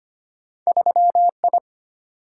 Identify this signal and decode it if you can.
Morse code, heard as a keyed tone, about 25 wpm, 700 Hz. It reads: 3I
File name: Two Letter-Number Combinations - 25wpm